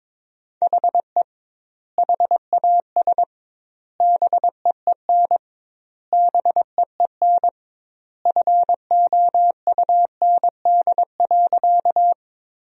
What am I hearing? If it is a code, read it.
Morse code, heard as a keyed tone, about 22 wpm, 700 Hz. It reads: HE HAS BEEN BEEN FOUND.
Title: 2x Word-Spacing - Sentences from Top 300 Words - 22wpm